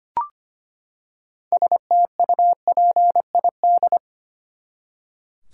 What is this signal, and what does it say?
Morse code, heard as a keyed tone, about 25 wpm, 700 Hz. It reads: STUPID
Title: Six Letter Words - 25wpm